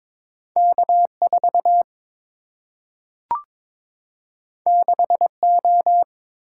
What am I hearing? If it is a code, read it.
Morse code, heard as a keyed tone, about 22 wpm, 700 Hz. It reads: K 4 6 O
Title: Single-Letter Single-Number 22wpm